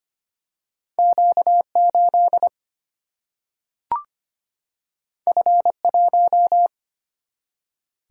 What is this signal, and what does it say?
Morse code, heard as a keyed tone, about 25 wpm, 700 Hz. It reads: Q8 F1
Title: Single-Letter Single-Number 25wpm